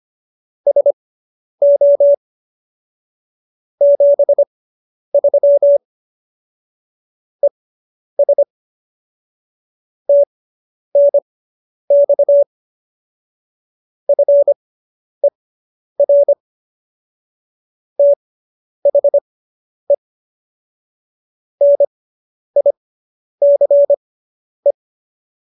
Morse code, heard as a keyed tone, about 25 wpm, spaced as for 10 wpm, 570 Hz. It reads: SO 73 ES TNX FER THE NICE